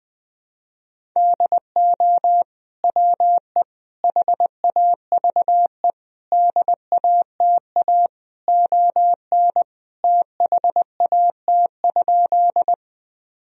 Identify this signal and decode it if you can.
Morse code, heard as a keyed tone, about 21 wpm, 700 Hz. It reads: DO WE HAVE DATA ON THAT?
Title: Sentences from Top 1000 Words - Review Part 3 - 20wpm